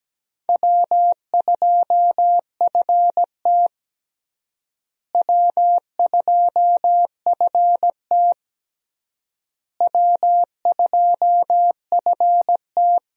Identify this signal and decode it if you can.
Morse code, heard as a keyed tone, about 17 wpm, 700 Hz. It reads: W2FT W2FT W2FT
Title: USA Call Signs - 4 Characters - Repeated 3x - 17wpm